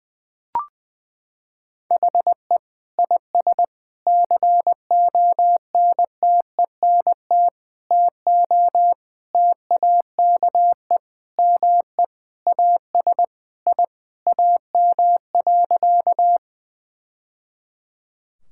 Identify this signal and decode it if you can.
Morse code, heard as a keyed tone, about 21 wpm, 700 Hz. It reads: HE IS CONTENT TO TAKE ME AS I AM.
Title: Sentences from Top 1000 Words - Review Part 6 - 20wpm